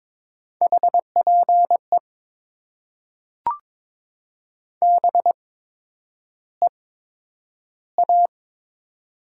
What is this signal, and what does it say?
Morse code, heard as a keyed tone, about 22 wpm, spaced as for 9 wpm, 700 Hz. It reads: HPE BEA
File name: Top 90 QSO Elements - 8x Character Spacing - 22wpm